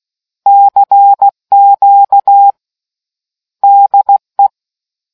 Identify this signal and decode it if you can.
Morse code, heard as a keyed tone, about 16 wpm, 800 Hz. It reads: CQ DE